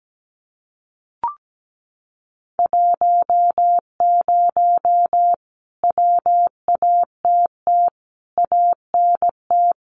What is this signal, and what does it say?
Morse code, heard as a keyed tone, about 17 wpm, 700 Hz. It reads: 10 WATT ANT
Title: Common QSO Phrases Encore 17wpm